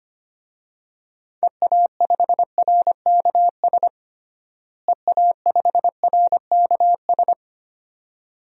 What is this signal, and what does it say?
Morse code, heard as a keyed tone, about 25 wpm, 700 Hz. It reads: EA5RKS EA5RKS
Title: DXCC-100 Call Signs - 6 Characters - Repeated 2x - 25wpm